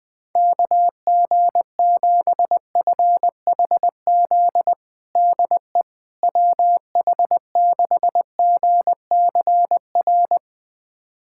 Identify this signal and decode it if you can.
Morse code, heard as a keyed tone, about 20 wpm, 700 Hz. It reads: KG7FHZ DE WH6GCR
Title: USA Callsign DE Callsign - 6 Characters - 20wpm